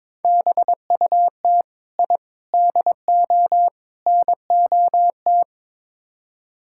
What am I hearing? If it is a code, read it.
Morse code, heard as a keyed tone, about 22 wpm, 700 Hz. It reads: BUT I DO NOT
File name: practice / 22wpm / Build A Phrase - 5-Word Phrases - 22wpm